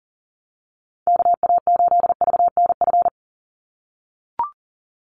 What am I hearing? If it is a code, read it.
Morse code, heard as a keyed tone, about 39 wpm, 700 Hz. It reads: KA8VNF
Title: USA Call Signs 40wpm